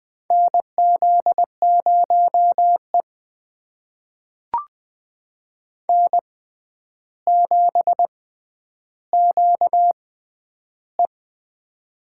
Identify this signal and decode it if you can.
Morse code, heard as a keyed tone, about 20 wpm, spaced as for 10 wpm, 700 Hz. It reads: NZ0E N7QE
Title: USA Call Signs - 4 Characters - 6x Character Spacing - 20wpm